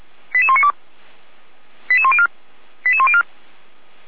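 Five-tone selective call (ZVEI), heard as quick runs of five tones, tones of about 70 ms. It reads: 88192 88104 88104
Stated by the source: Five-tone ZVEI selective call received on SDR.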